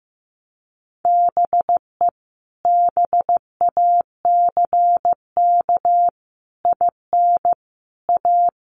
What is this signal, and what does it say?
Morse code, heard as a keyed tone, about 15 wpm, 700 Hz. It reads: BE BACK IN A
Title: Sets of 5 Words 15wpm